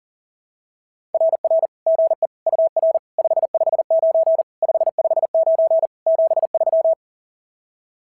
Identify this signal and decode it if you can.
Morse code, heard as a keyed tone, about 44 wpm, 660 Hz. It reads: RR GE UR 559 559 73
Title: SOTA QSO Practice 40wpm